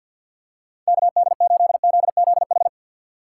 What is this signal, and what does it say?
Morse code, heard as a keyed tone, about 52 wpm, 700 Hz. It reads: KD8ZZH
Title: USA Call Signs - 6 Characters - 6x Character Spacing - 50wpm